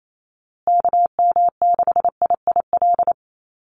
Morse code, heard as a keyed tone, about 28 wpm, 700 Hz. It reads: KM6IIL
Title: USA Call Signs Repeated 2x 28wpm